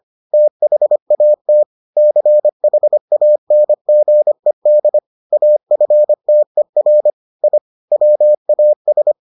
Morse code, heard as a keyed tone, about 25 wpm, 600 Hz. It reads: THAT CHANGED AFTER I WAS